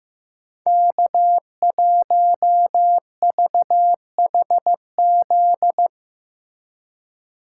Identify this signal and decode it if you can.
Morse code, heard as a keyed tone, about 15 wpm, 700 Hz. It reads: K1VHZ
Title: USA Call Signs - 2x Character Spacing - 15wpm